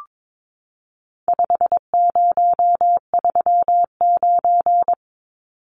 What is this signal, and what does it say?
Morse code, heard as a keyed tone, about 22 wpm, 700 Hz. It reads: 5039
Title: Numbers - 4-Digit 22wpm